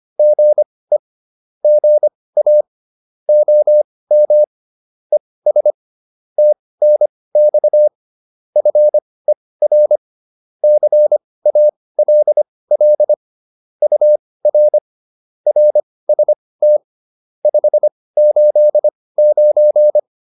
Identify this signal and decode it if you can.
Morse code, heard as a keyed tone, about 25 wpm, spaced as for 18 wpm, 600 Hz. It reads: GE GA OM ES TNX FER CALL UR RST 589
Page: intermediate-practice-files